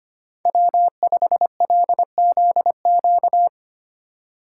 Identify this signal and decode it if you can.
Morse code, heard as a keyed tone, about 27 wpm, 700 Hz. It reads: W5LZQ
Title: Speed Racing - USA 1x3 Call Signs 25wpm